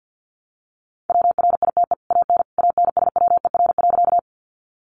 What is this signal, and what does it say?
Morse code, heard as a keyed tone, about 50 wpm, 700 Hz. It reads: WRITE AN ANSWER.
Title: Sentences from Top 300 Words 50wpm